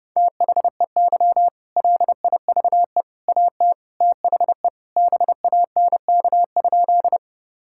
Morse code, heard as a keyed tone, about 30 wpm, 700 Hz. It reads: THEY LIVE AT THE BANK?
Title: Sentences from Top 1000 Words - Review Part 6 - 30wpm